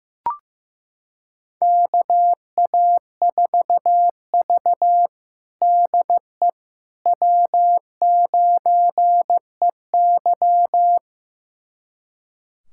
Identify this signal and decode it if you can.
Morse code, heard as a keyed tone, about 15 wpm, 700 Hz. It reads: KA4V DE W9EY
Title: USA Callsign DE Callsign - 4 Characters - 15wpm